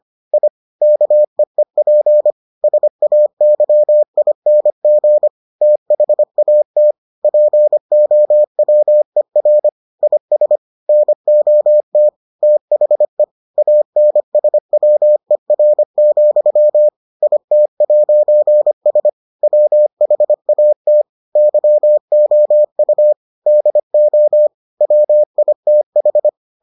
Morse code, heard as a keyed tone, about 25 wpm, 600 Hz. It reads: I KEEP SAYING THAT POWER IS NOT THE ANSWER, IT'S WHAT YOU DO WITH